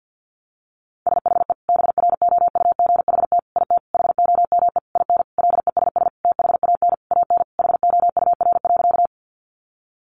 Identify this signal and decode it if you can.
Morse code, heard as a keyed tone, about 50 wpm, 700 Hz. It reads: SHE BROUGHT IT HOME IN LESS THAN AN HOUR.
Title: Sentences from Top 500 Words 50wpm